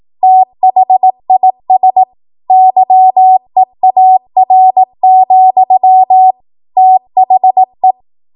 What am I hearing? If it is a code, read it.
Morse code, heard as a keyed tone, about 18 wpm, 750 Hz. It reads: THIS YEAR, THE
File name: Morse / 18WPM